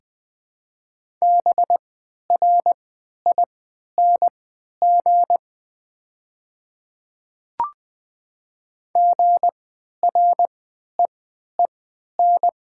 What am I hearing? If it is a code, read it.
Morse code, heard as a keyed tone, about 20 wpm, spaced as for 11 wpm, 700 Hz. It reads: BRING GREEN
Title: 3x Character Spacing - Top 500 Words - 20wpm